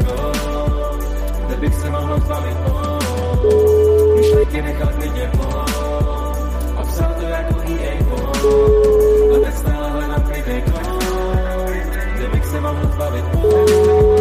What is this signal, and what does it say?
Ringback tone (Europe, 425 Hz), with noise.